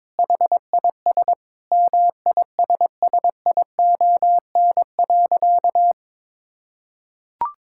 Morse code, heard as a keyed tone, about 22 wpm, 700 Hz. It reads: HIS MISSION.
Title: Sentences from Top 1600 Words 22wpm